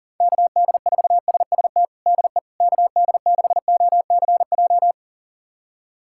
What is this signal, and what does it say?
Morse code, heard as a keyed tone, about 40 wpm, 700 Hz. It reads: KD4SST DE KD6OCJ
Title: USA Callsign DE Callsign - 6 Characters - 40wpm